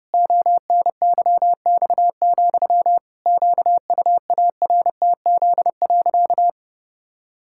Morse code, heard as a keyed tone, about 32 wpm, 700 Hz. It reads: ONYX, QUARTZ.
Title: Pangram Challenge v2 - 30wpm